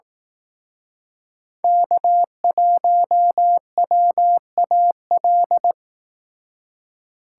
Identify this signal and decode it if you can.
Morse code, heard as a keyed tone, about 18 wpm, 700 Hz. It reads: K1WAL
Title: Speed Racing - USA Call Signs 15wpm